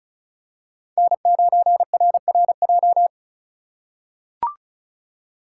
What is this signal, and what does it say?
Morse code, heard as a keyed tone, about 36 wpm, 700 Hz. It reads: N9RRJ
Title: USA Call Signs - 3x Character Spacing - 35wpm